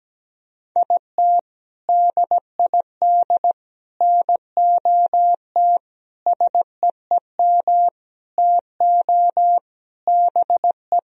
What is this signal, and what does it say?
Morse code, heard as a keyed tone, about 17 wpm, 700 Hz. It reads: IT DID NOT SEEM TO BE
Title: Sentences from Top 1000 Words - Review Part 1 - 17wpm